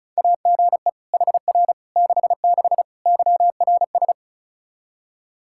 Morse code, heard as a keyed tone, about 36 wpm, 700 Hz. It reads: AGE HR 66 YRS
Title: Practice QSO - Exchange 3 - More Info - 35wpm